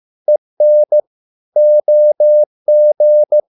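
Morse code, heard as a keyed tone, about 15 wpm, 600 Hz. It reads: EN OG
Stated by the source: Morse code at a slow 15 wpm from CW med Gnister.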